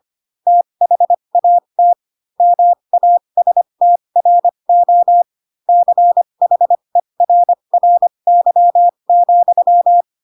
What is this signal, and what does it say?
Morse code, heard as a keyed tone, about 25 wpm, spaced as for 22 wpm, 700 Hz. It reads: THAT MASTRO CHERRY,